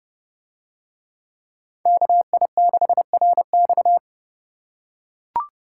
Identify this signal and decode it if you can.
Morse code, heard as a keyed tone, about 30 wpm, 700 Hz. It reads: KI6RX